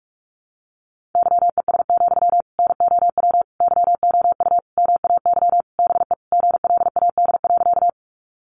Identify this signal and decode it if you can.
Morse code, heard as a keyed tone, about 44 wpm, 700 Hz. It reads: YES, NOW YOU MAY BE GLAD.
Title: Sentences from Top 800 Words 45wpm